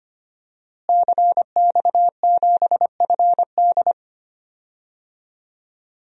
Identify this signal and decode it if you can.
Morse code, heard as a keyed tone, about 25 wpm, 700 Hz. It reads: CX7FD
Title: DXCC-100 Call Signs - 5 Characters - Repeated 2x - 25wpm